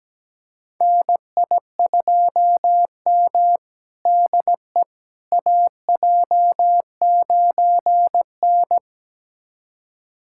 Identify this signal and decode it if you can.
Morse code, heard as a keyed tone, about 17 wpm, 700 Hz. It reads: NI2M DE AJ9N
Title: USA Callsign DE Callsign - 4 Characters - 17wpm